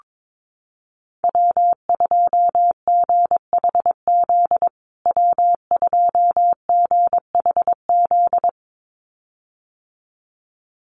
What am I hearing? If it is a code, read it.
Morse code, heard as a keyed tone, about 22 wpm, 700 Hz. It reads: W2GHZ W2GHZ
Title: USA Call Signs Repeated 2x 22wpm